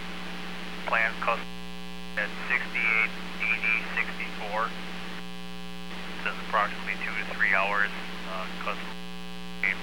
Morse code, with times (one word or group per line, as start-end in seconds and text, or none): none